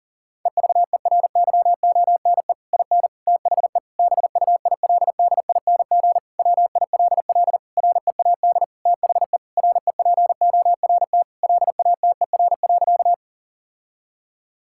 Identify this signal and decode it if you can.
Morse code, heard as a keyed tone, about 37 wpm, 700 Hz. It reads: EVERYONE IN THE BUILDING WILL READ THE REPORT LATER.
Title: Sentences from Top 1000 Words - Review Part 6 - 40wpm